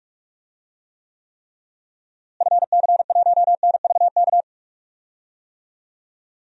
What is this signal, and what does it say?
Morse code, heard as a keyed tone, about 46 wpm, 700 Hz. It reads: FC1NVK
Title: DXCC-100 Call Signs - 6 Characters - Repeated 3x - 45wpm